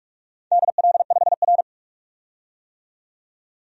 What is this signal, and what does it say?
Morse code, heard as a keyed tone, about 42 wpm, 700 Hz. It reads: DL5R
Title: DXCC-100 Call Signs - 4 Characters - 45wpm